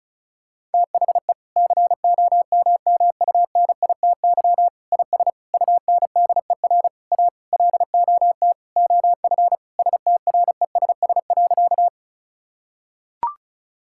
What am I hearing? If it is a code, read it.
Morse code, heard as a keyed tone, about 34 wpm, 700 Hz. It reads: THE COMMUNITY IS UNDER A LOT OF STRESS.
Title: Sentences from Top 1000 Words - Review Part 6 - 35wpm